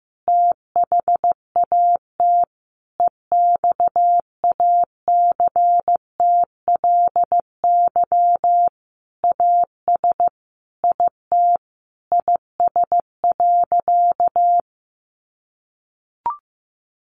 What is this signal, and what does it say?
Morse code, heard as a keyed tone, about 15 wpm, 700 Hz. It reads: THAT EXACTLY AS IT IS.
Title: Sentences from Top 900 Words 15wpm